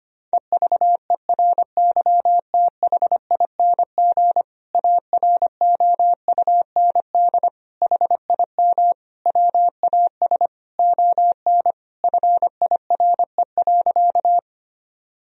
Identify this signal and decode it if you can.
Morse code, heard as a keyed tone, about 25 wpm, 700 Hz. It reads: EVERYTHING AROUND HIM WAS ON FIRE.
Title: Sentences from Top 1000 Words - Review Part 3 - 25wpm